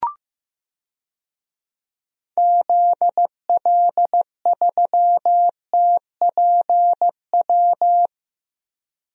Morse code, heard as a keyed tone, about 15 wpm, 700 Hz. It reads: ZL3TPW
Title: DXCC-100 Call Signs - 6 Characters - Repeated 3x - 15wpm